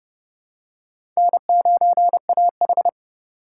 Morse code, heard as a keyed tone, about 30 wpm, 690 Hz. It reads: N9AH
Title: USA Call Signs - 4 Characters - 2x Character Spacing - 30wpm